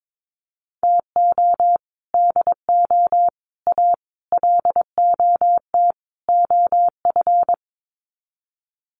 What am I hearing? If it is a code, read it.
Morse code, heard as a keyed tone, about 22 wpm, 700 Hz. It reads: TO DO A LOT OF
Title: Sets of 5 Words Encore 22wpm